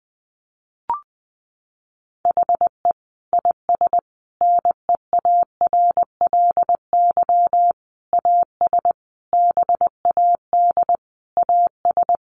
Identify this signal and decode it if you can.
Morse code, heard as a keyed tone, about 19 wpm, 700 Hz. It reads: HE IS NEARLY AS BAD AS
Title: Sentences from Top 1300 Words 20wpm